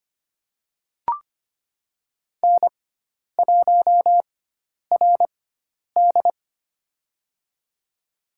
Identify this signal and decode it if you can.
Morse code, heard as a keyed tone, about 25 wpm, spaced as for 10 wpm, 700 Hz. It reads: N1RD
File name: Lesson 16g - Letter c - Call Signs 25wpm